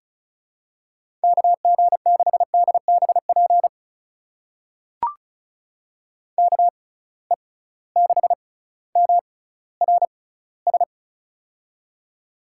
Morse code, heard as a keyed tone, about 36 wpm, spaced as for 18 wpm, 700 Hz. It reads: KG6DBP KE6MRS
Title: USA Call Signs - 6 Characters - 6x Character Spacing - 35wpm